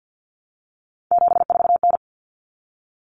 Morse code, heard as a keyed tone, about 50 wpm, 700 Hz. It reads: 74N
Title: Common Cut Numbers - 3-Digit 50wpm